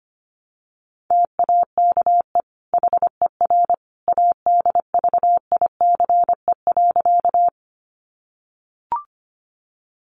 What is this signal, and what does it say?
Morse code, heard as a keyed tone, about 25 wpm, 700 Hz. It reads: TAKE HER ADVICE.
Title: Sentences from Top 1100 Words 25wpm